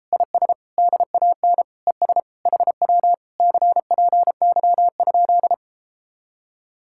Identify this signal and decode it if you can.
Morse code, heard as a keyed tone, about 33 wpm, 700 Hz. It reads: IS DAN ES HW CPY?